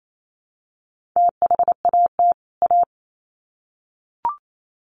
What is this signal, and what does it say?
Morse code, heard as a keyed tone, about 28 wpm, 700 Hz. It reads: THAT A
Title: Sets of 2 Words 28wpm